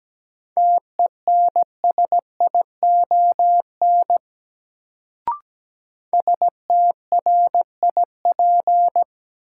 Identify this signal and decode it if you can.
Morse code, heard as a keyed tone, about 17 wpm, 700 Hz. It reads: TENSION STRIP
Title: Sentences from Top 1600 Words - New Word Drill Alt - 17wpm